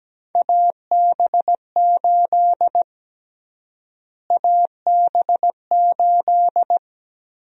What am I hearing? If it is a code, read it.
Morse code, heard as a keyed tone, about 17 wpm, 700 Hz. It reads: AB8 AB8
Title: USA Call Sign Prefixes - Repeated 2x - 17wpm